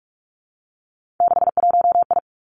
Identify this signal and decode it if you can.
Morse code, heard as a keyed tone, about 46 wpm, 700 Hz. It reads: 61I
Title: Three Challenging Letter-Number Combinations 45wpm